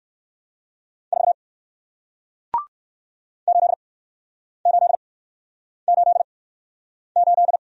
Morse code, heard as a keyed tone, about 57 wpm, spaced as for 9 wpm, 700 Hz. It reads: 4 8888